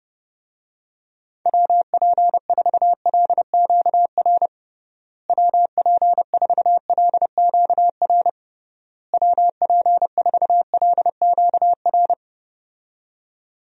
Morse code, heard as a keyed tone, about 30 wpm, 700 Hz. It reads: WP4LQR WP4LQR WP4LQR
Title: USA Call Signs - 6 Characters - Repeated 3x - 30wpm